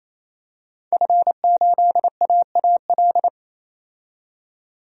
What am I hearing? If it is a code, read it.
Morse code, heard as a keyed tone, about 28 wpm, 700 Hz. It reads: F8AAL
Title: DXCC-100 Call Signs - 5 Characters - 28wpm